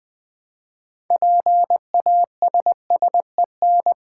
Morse code, heard as a keyed tone, about 20 wpm, 700 Hz. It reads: PASSEN
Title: Build A Word - 3 Letters At A Time - 9-Letter Words - 20wpm